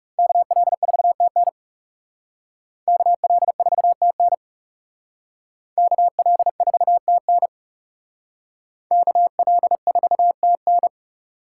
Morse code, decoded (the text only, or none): KL4TN KL4TN KL4TN KL4TN